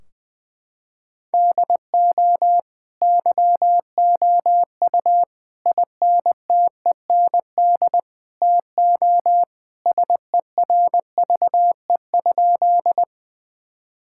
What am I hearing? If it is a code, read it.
Morse code, heard as a keyed tone, about 21 wpm, 700 Hz. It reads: DO YOU INTEND TO SERVE?
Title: Sentences from Top 1600 Words 20wpm